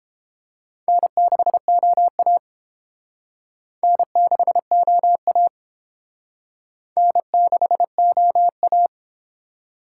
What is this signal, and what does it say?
Morse code, heard as a keyed tone, about 29 wpm, 700 Hz. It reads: N6OA N6OA N6OA